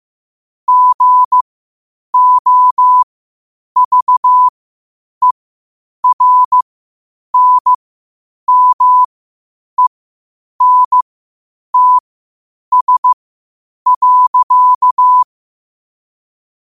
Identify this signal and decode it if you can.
Morse code, heard as a keyed tone, about 15 wpm, spaced as for 8 wpm, 1000 Hz. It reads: GOVERNMENTS.
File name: qotdfiles / 8wpm